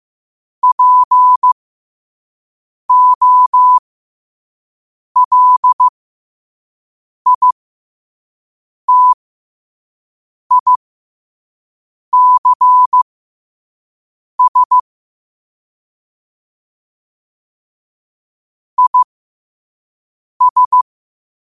Morse code, heard as a keyed tone, about 15 wpm, spaced as for 5 wpm, 1000 Hz. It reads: POLITICS IS
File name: qotdfiles / 5wpm